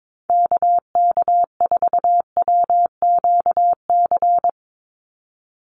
Morse code, heard as a keyed tone, about 22 wpm, 700 Hz. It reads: KK4WQC
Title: USA Call Signs 22wpm